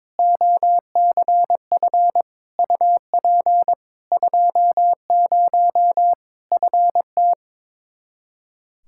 Morse code, heard as a keyed tone, about 22 wpm, 700 Hz. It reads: OCF UP 20 FT
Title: Practice QSO - Exchange 2 - Info - 22wpm